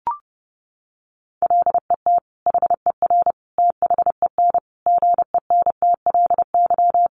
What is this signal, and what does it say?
Morse code, heard as a keyed tone, about 30 wpm, 700 Hz. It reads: LET HER THEN GENTLY
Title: Sentences from Top 1500 Words 30wpm